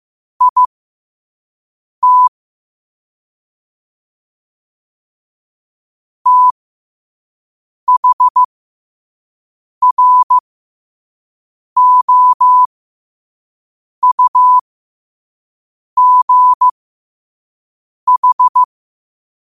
Morse code, heard as a keyed tone, about 15 wpm, spaced as for 5 wpm, 1000 Hz. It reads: IT THROUGH
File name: qotdfiles / 5wpm